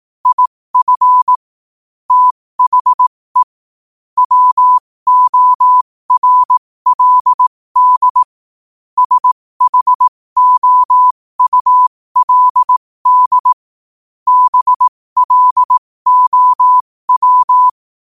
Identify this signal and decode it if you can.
Morse code, heard as a keyed tone, about 18 wpm, spaced as for 15 wpm, 1000 Hz. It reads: IF THE WORLD SHOULD BLOW
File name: qotdfiles / 15wpm